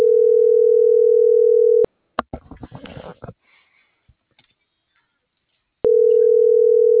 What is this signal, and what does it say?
Ringback tone (North America), 440 + 480 Hz.